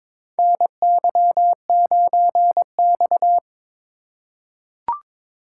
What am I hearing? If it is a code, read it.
Morse code, heard as a keyed tone, about 22 wpm, 700 Hz. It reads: NY9X
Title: K1USN SST QSO Practice - 22wpm